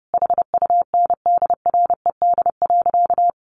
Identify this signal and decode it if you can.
Morse code, heard as a keyed tone, about 30 wpm, 700 Hz. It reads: HUNDRED.
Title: Sentences from Top 1300 Words 30wpm